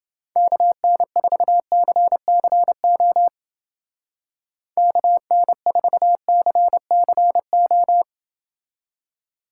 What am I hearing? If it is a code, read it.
Morse code, heard as a keyed tone, about 29 wpm, 700 Hz. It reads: KN4CCO KN4CCO